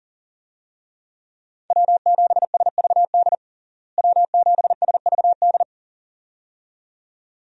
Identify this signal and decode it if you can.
Morse code, heard as a keyed tone, about 37 wpm, 700 Hz. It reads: W7SVD W7SVD
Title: USA Call Signs - 5 Characters - Repeated 2x - 40wpm